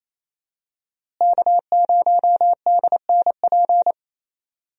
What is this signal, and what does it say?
Morse code, heard as a keyed tone, about 28 wpm, 700 Hz. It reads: K0DNP
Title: USA Call Signs - 5 Characters - Repeated 2x - 28wpm